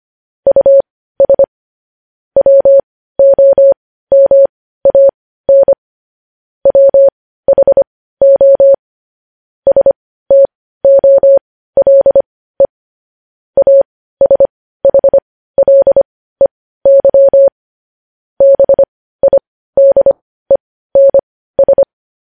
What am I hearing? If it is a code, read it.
Morse code, heard as a keyed tone, about 25 wpm, spaced as for 15 wpm, 550 Hz. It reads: US WOMAN WHO STOLE ASHLEY BIDENS